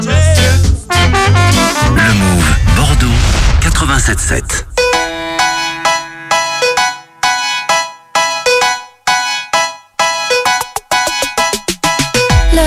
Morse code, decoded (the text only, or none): none